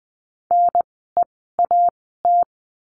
Morse code, heard as a keyed tone, about 20 wpm, spaced as for 15 wpm, 700 Hz. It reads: NEAT
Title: 036 Lesson 7f - Letter s - Words 20wpm